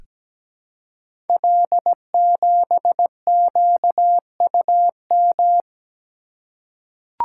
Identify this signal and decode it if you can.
Morse code, heard as a keyed tone, about 17 wpm, 700 Hz. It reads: L7QUM
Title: Code Groups - 5 Letter:Numbers - 17wpm